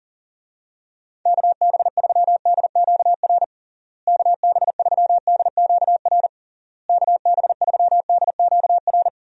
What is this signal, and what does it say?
Morse code, heard as a keyed tone, about 40 wpm, 700 Hz. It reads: KB3DQR KB3DQR KB3DQR
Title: USA Call Signs - 6 Characters - Repeated 3x - 40wpm